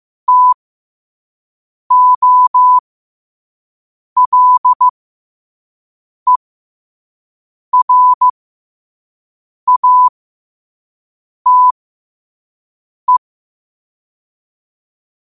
Morse code, heard as a keyed tone, about 15 wpm, spaced as for 5 wpm, 1000 Hz. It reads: TOLERATE
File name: qotdfiles / 5wpm